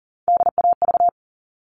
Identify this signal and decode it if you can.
Morse code, heard as a keyed tone, about 40 wpm, 700 Hz. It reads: DAV
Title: All Cut Numbers - 3-Digit 40wpm